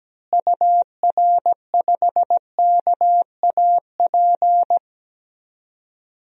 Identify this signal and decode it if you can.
Morse code, heard as a keyed tone, about 17 wpm, 700 Hz. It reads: UR5KAP
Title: DXCC-100 Call Signs - 6 Characters - Repeated 3x - 17wpm